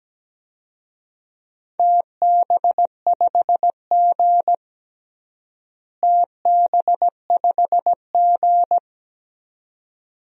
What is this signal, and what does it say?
Morse code, heard as a keyed tone, about 17 wpm, 700 Hz. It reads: TB5G TB5G
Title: DXCC-100 Call Signs - 4 Characters - Repeated 2x - 17wpm